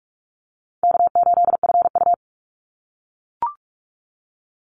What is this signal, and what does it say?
Morse code, heard as a keyed tone, about 44 wpm, 700 Hz. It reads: K8FU
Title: Easy Call Signs - 45wpm